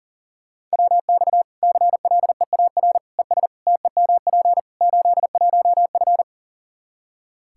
Morse code, heard as a keyed tone, about 40 wpm, 700 Hz. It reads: WX CLEAR ES TEMP 81F
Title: Practice QSO - Exchange 2 - Info - 40wpm